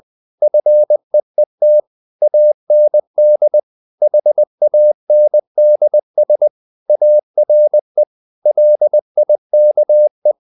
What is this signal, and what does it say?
Morse code, heard as a keyed tone, about 20 wpm, 600 Hz. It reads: FEET AND HANDS ARE LIKE
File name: the old man and the sea / the old man and the sea part 1 20wpm